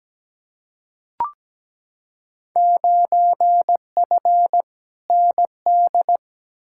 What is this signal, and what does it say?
Morse code, heard as a keyed tone, about 17 wpm, 700 Hz. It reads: 9F ND
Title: Field Day Exchanges Encore - 17wpm